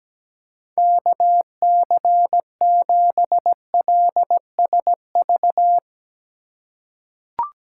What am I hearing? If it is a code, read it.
Morse code, heard as a keyed tone, about 17 wpm, 700 Hz. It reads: KC7LSV
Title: USA Call Signs - 6 Characters - 3x Character Spacing - 17wpm